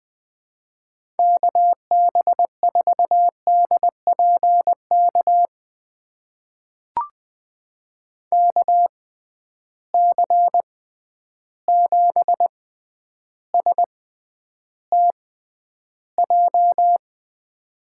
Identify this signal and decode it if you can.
Morse code, heard as a keyed tone, about 20 wpm, spaced as for 10 wpm, 700 Hz. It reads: KB4DPK KC7STJ
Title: USA Call Signs - 6 Characters - 6x Character Spacing - 20wpm